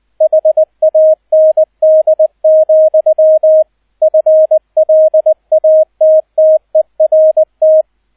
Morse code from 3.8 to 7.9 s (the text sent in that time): FLATTERT